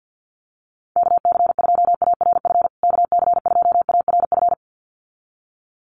Morse code, heard as a keyed tone, about 50 wpm, 700 Hz. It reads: KC2ARF KC2ARF
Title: USA Call Signs Repeated 2x 50wpm